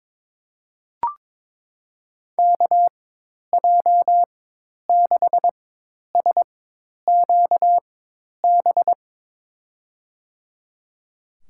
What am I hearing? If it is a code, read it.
Morse code, heard as a keyed tone, about 22 wpm, spaced as for 10 wpm, 700 Hz. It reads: KJ6SQB
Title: USA Call Signs - 6 Characters - 4x Character Spacing - 22wpm